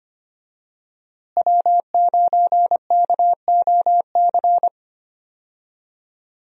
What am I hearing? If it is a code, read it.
Morse code, heard as a keyed tone, about 25 wpm, 700 Hz. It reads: W9KOC
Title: USA Call Signs - 5 Characters - Repeated 3x - 25wpm